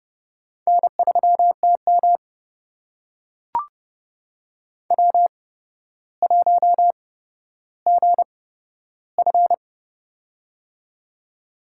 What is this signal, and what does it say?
Morse code, heard as a keyed tone, about 30 wpm, spaced as for 13 wpm, 700 Hz. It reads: N3TM W1GF
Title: USA Call Signs - 4 Characters - 8x Character Spacing - 30wpm